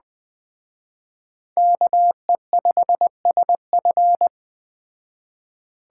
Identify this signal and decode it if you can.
Morse code, heard as a keyed tone, about 20 wpm, 700 Hz. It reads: KE5SF
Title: Speed Racing - USA 2x2 Call Signs 17wpm